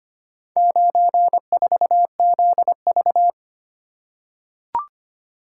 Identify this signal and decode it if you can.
Morse code, heard as a keyed tone, about 25 wpm, 700 Hz. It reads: 94ZV
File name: practice / 25wpm / Four Letter-Number Combinations - 25wpm